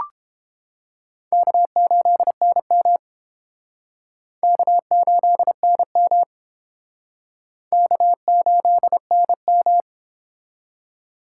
Morse code, heard as a keyed tone, about 30 wpm, 700 Hz. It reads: K8NM K8NM K8NM